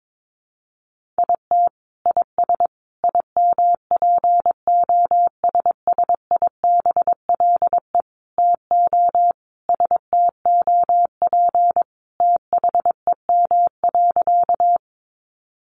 Morse code, heard as a keyed tone, about 22 wpm, 700 Hz. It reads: IT IS IMPOSSIBLE TO STOP THEM.
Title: Sentences from Top 1100 Words 22wpm